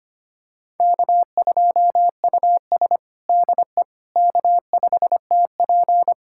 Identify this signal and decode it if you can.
Morse code, heard as a keyed tone, about 25 wpm, 700 Hz. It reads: K2US DE K5TP
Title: USA Callsign DE Callsign - 4 Characters - 25wpm